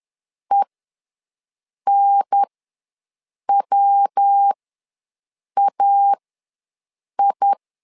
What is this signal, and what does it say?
Morse code, heard as a keyed tone, about 10 wpm, spaced as for 6 wpm, 800 Hz. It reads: ENWAI
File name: LEZIONI / 3 MWRO / 3 MW ws 10wpm 2213